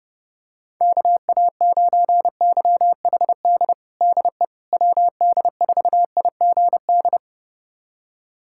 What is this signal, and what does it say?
Morse code, heard as a keyed tone, about 30 wpm, 700 Hz. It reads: KA9YHD DE WD4IGD
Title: USA Callsign DE Callsign 30wpm